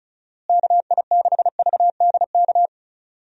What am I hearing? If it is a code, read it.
Morse code, heard as a keyed tone, about 34 wpm, 700 Hz. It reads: KI6VDK
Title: USA Call Signs - 6 Characters - Repeated 2x - 35wpm